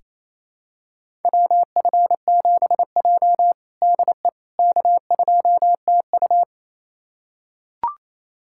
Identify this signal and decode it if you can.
Morse code, heard as a keyed tone, about 28 wpm, 700 Hz. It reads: WF7J DE K2TU
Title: USA Callsign DE Callsign - 4 Characters - 28wpm